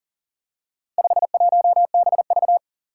Morse code, heard as a keyed tone, about 40 wpm, 700 Hz. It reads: 51BV
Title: Four Challenging Letter-Number Combinations 40wpm